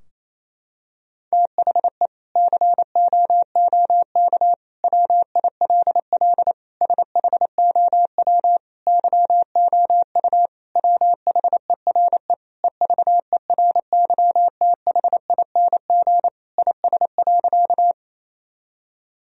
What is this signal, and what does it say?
Morse code, heard as a keyed tone, about 28 wpm, 700 Hz. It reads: THE COOK WILL SHOW YOU WHERE EVERYTHING IS.
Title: Sentences from Top 1000 Words - Review Part 4 - 28wpm